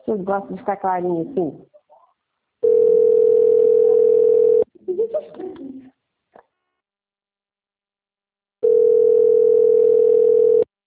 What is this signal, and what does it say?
Ringback tone (North America), 440 + 480 Hz, with speech.
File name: rg-1600-unknown- - - .4424